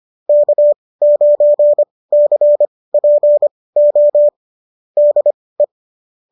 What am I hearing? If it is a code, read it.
Morse code, heard as a keyed tone, about 25 wpm, spaced as for 18 wpm, 590 Hz. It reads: K9CPO DE